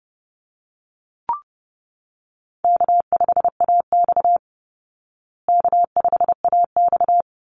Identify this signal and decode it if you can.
Morse code, heard as a keyed tone, about 30 wpm, 700 Hz. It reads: K5AX K5AX
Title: Easy Call Signs Repeated 2x - 30wpm